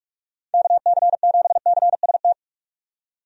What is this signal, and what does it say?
Morse code, heard as a keyed tone, about 43 wpm, 700 Hz. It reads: KC7CST
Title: USA Call Signs - 6 Characters - 45wpm